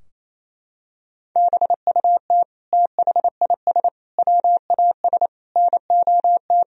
Morse code, heard as a keyed tone, about 28 wpm, 700 Hz. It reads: BUT THIS WAS NOT
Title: Sentences from Top 1000 Words - Review Part 4 - 28wpm